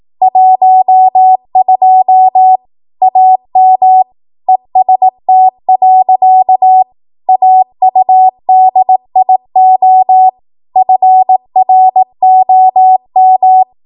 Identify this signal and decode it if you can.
Morse code, heard as a keyed tone, about 18 wpm, 750 Hz. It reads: 12 AM EST. AUDIO FROM
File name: Archive / 18 WPM / 18WPM